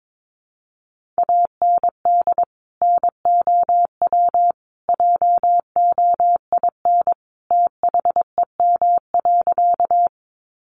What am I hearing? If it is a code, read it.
Morse code, heard as a keyed tone, about 22 wpm, 700 Hz. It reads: AND NOW JOIN THEM.
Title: Sentences from Top 1400 Words 22wpm